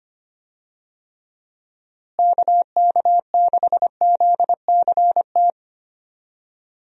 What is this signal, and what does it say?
Morse code, heard as a keyed tone, about 25 wpm, 700 Hz. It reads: KK6ZCT
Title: USA Call Signs - 6 Characters - Repeated 3x - 25wpm